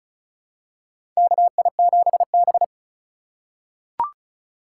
Morse code, heard as a keyed tone, about 36 wpm, 700 Hz. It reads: KI7B
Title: USA Call Signs - 4 Characters - 3x Character Spacing - 35wpm